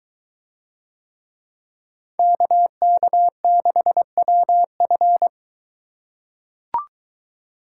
Morse code, heard as a keyed tone, about 21 wpm, 700 Hz. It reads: KK6WF